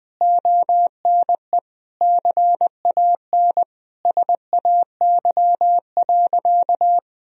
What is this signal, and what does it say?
Morse code, heard as a keyed tone, about 19 wpm, 700 Hz. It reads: ONE CAN SAY.
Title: Sentences from Top 1000 Words - Review Part 2 - 20wpm